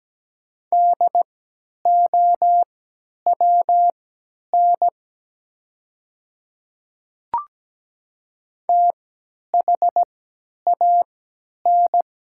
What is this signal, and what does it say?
Morse code, heard as a keyed tone, about 17 wpm, spaced as for 10 wpm, 700 Hz. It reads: DOWN THAN
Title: 3x Character Spacing - Top 100 Words - 17wpm